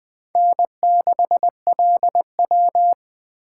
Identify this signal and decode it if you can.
Morse code, heard as a keyed tone, about 20 wpm, 700 Hz. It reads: N6LW